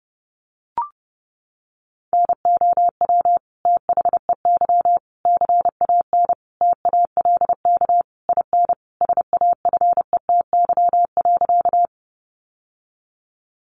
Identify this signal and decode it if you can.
Morse code, heard as a keyed tone, about 30 wpm, 700 Hz. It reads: NOW THEY CAN TALK IN SAFETY.
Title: Sentences from Top 700 Words 30wpm